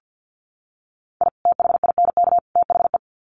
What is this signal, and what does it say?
Morse code, heard as a keyed tone, about 50 wpm, 700 Hz. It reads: I THINK THE
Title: Sets of 3 Words 50wpm